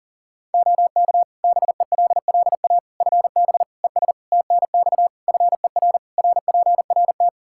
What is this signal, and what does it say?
Morse code, heard as a keyed tone, about 40 wpm, 700 Hz. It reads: OK BELLA FB ES TNX FER RPRT
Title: K1USN SST QSO Practice 40wpm